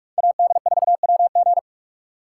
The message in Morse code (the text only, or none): AD4WG